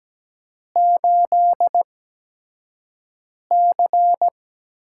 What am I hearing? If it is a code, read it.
Morse code, heard as a keyed tone, about 17 wpm, spaced as for 5 wpm, 700 Hz. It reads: 8C
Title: USA Call Signs - 6 Characters - 8x Character Spacing - 17wpm